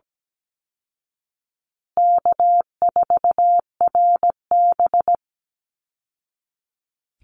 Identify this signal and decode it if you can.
Morse code, heard as a keyed tone, about 17 wpm, 700 Hz. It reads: K4RB
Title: Easy Call Signs Repeated 3x - 17wpm